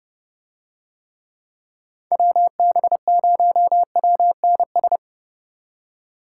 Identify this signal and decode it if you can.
Morse code, heard as a keyed tone, about 30 wpm, 700 Hz. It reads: WB0WNS